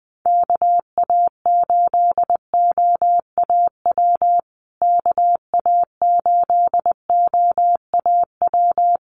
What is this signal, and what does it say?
Morse code, heard as a keyed tone, about 20 wpm, 700 Hz. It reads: KA8OAW KA8OAW
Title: USA Call Signs Repeated 3x 20wpm